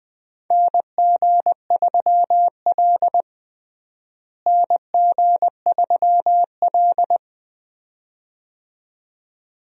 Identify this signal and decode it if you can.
Morse code, heard as a keyed tone, about 20 wpm, 700 Hz. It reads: NG3L NG3L
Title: USA Call Signs - 4 Characters - Repeated 2x - 20wpm